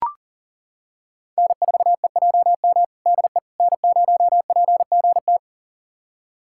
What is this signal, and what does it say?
Morse code, heard as a keyed tone, about 40 wpm, 700 Hz. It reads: N4EJM DE N0PGT
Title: USA Callsign DE Callsign - 5 Characters - 40wpm